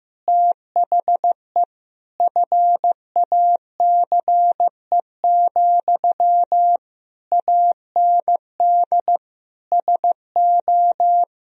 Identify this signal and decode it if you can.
Morse code, heard as a keyed tone, about 15 wpm, 700 Hz. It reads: THE FACE, AND SO
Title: Sentences from Top 1000 Words - Review Part 2 - 15wpm